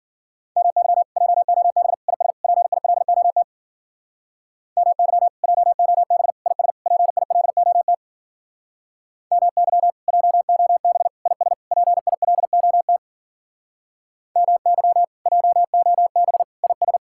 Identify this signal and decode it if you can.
Morse code, heard as a keyed tone, about 52 wpm, 700 Hz. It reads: MY JOB IS PILOT MY JOB IS PILOT MY JOB IS PILOT MY JOB IS